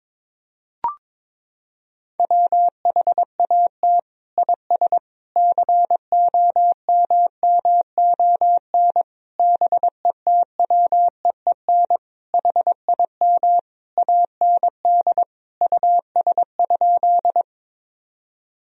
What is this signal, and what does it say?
Morse code, heard as a keyed tone, about 22 wpm, 700 Hz. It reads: WHAT IS COMMON BETWEEN HIM AND US?
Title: Sentences from Top 1000 Words - Review Part 3 - 22wpm